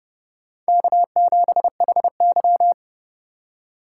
Morse code, heard as a keyed tone, about 28 wpm, 700 Hz. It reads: K7HY